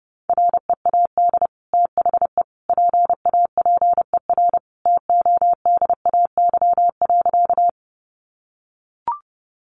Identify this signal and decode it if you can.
Morse code, heard as a keyed tone, about 30 wpm, 700 Hz. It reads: READ THE PAPER TODAY.
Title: Sentences from Top 600 Words 30wpm